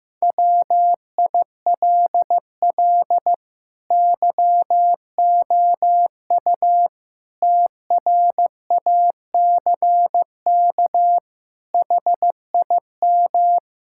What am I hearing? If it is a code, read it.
Morse code, heard as a keyed tone, about 15 wpm, 700 Hz. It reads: WILL YOU TRACK HIM
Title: Sentences from Top 1600 Words 15wpm